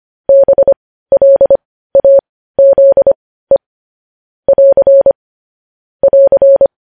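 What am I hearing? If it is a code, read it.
Morse code, heard as a keyed tone, about 25 wpm, spaced as for 15 wpm, 550 Hz. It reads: BLAZE <AR> <AR>
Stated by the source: Morse news headlines practice at 15 wpm.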